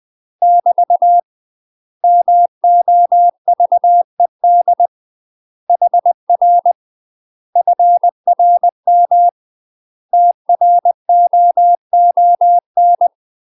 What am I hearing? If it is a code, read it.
Morse code, heard as a keyed tone, about 20 wpm, 700 Hz. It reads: <BT> MOVED HR FRM TROON
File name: UK MASTER QSOs TRAINING / 20WPM / UK MASTER QSOs 0026